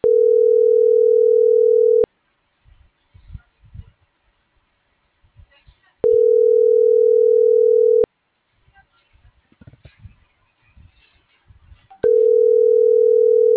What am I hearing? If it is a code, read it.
Ringback tone (North America), 440 + 480 Hz.